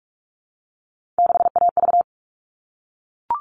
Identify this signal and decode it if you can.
Morse code, heard as a keyed tone, about 44 wpm, 700 Hz. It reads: 6AV
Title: All Cut Numbers - 3-Digit 45wpm